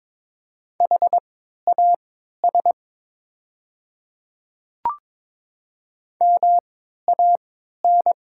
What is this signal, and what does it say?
Morse code, heard as a keyed tone, about 22 wpm, spaced as for 12 wpm, 700 Hz. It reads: HAS MAN
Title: 3x Character Spacing - Top 200 Words - 22wpm